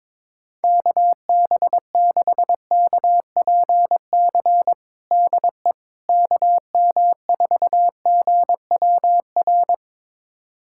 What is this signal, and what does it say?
Morse code, heard as a keyed tone, about 21 wpm, 700 Hz. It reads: KB6KPC DE KM4GWR
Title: USA Callsign DE Callsign - 6 Characters - 22wpm